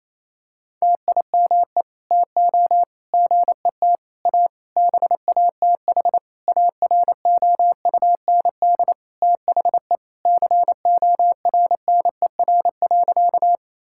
Morse code, heard as a keyed tone, about 28 wpm, 700 Hz. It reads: TIME TO GET A BATH AROUND THE CORNER.
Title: Sentences from Top 1600 Words 28wpm